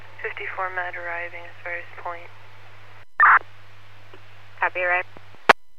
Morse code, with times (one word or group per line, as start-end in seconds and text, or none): none